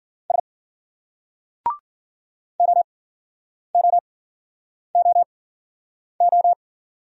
Morse code, heard as a keyed tone, about 56 wpm, spaced as for 9 wpm, 700 Hz. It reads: S OOOO